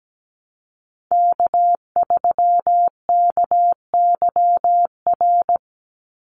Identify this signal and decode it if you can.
Morse code, heard as a keyed tone, about 17 wpm, 700 Hz. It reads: K3KYR
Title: NA SOTA Chaser Callsigns 17wpm